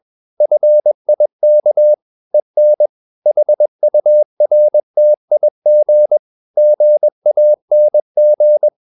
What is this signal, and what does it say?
Morse code, heard as a keyed tone, about 21 wpm, 600 Hz. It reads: FIK EN HURTIG GANG
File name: Ferie QRS 21wpm